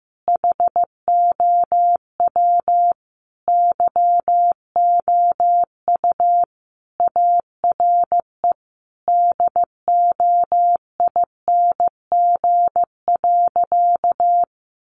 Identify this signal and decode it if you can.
Morse code, heard as a keyed tone, about 15 wpm, 700 Hz. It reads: HOW YOU ARE DOING.
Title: Sentences from Top 1400 Words 15wpm